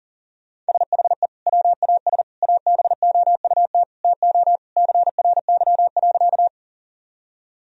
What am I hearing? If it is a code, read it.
Morse code, heard as a keyed tone, about 40 wpm, 700 Hz. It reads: SHE WAS ABOUT TO CRY.
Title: Sentences from Top 1000 Words - Review Part 2 - 40wpm